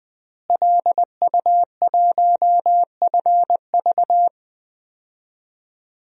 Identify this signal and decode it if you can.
Morse code, heard as a keyed tone, about 20 wpm, 700 Hz. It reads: LU1FV
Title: DXCC-100 Call Signs - 5 Characters - Repeated 3x - 20wpm